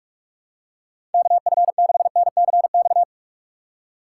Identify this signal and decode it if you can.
Morse code, heard as a keyed tone, about 44 wpm, 700 Hz. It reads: KF6NCX
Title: K1USN SST QSO Practice - 45wpm